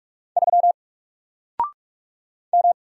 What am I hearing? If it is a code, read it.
Morse code, heard as a keyed tone, about 42 wpm, 700 Hz. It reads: 3 M
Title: Single Character (Letter:Number:Punctuation:Prosign) Alt 45wpm